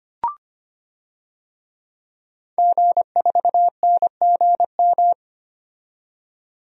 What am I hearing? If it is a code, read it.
Morse code, heard as a keyed tone, about 25 wpm, 700 Hz. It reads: G4NGM
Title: DXCC-100 Call Signs - 5 Characters - 25wpm